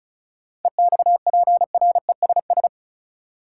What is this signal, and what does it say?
Morse code, heard as a keyed tone, about 36 wpm, 700 Hz. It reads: EXPRESS
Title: Sentences from Top 1100 Words - New Word Drill - 35wpm